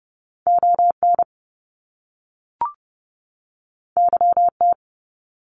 Morse code, heard as a keyed tone, about 30 wpm, 700 Hz. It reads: ON YT
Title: Canadian Provinces and Territories 30wpm